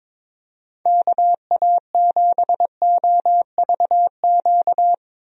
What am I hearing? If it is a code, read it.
Morse code, heard as a keyed tone, about 22 wpm, 700 Hz. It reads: KA7OVQ
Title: USA Call Signs - 6 Characters - 22wpm